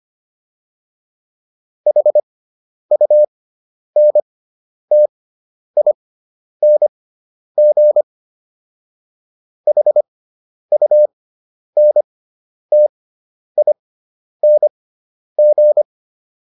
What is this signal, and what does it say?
Morse code, heard as a keyed tone, about 25 wpm, spaced as for 10 wpm, 600 Hz. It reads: HUNTING HUNTING